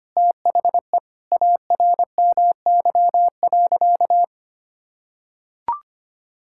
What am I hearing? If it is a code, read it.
Morse code, heard as a keyed tone, about 25 wpm, 700 Hz. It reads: THE ARMY.
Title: Sentences from Top 1000 Words - Review Part 6 - 25wpm